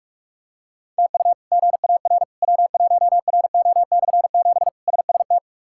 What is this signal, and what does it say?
Morse code, heard as a keyed tone, about 46 wpm, 700 Hz. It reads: TU GAR W1RO/7 SST
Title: K1USN SST QSO Practice - 45wpm